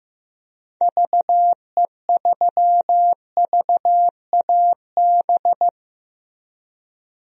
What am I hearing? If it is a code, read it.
Morse code, heard as a keyed tone, about 15 wpm, 700 Hz. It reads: VE3VAB
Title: DXCC-100 Call Signs - 6 Characters - Repeated 3x - 15wpm